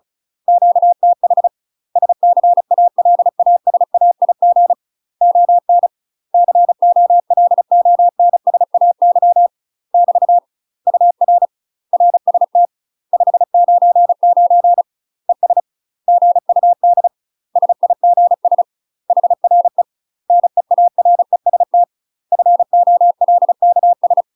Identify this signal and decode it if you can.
Morse code, heard as a keyed tone, about 34 wpm, 700 Hz. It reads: QTH SCALASAIG ON COLONSAY <BT> UR RST 599 ES GUD SIGS HRE NEAREST FOLKS
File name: UK MASTER QSOs TRAINING / 35WPM / UK MASTER QSOs 0000